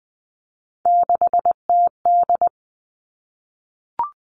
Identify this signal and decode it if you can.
Morse code, heard as a keyed tone, about 20 wpm, 700 Hz. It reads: 6TD
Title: All Cut Numbers - 3-Digit 20wpm